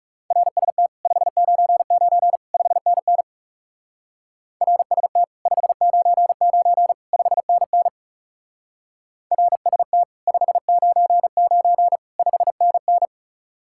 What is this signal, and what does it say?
Morse code, heard as a keyed tone, about 34 wpm, 700 Hz. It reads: RST 599 5NN RST 599 5NN RST 599 5NN
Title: Speed-Racing Common QSO Phrases - 30wpm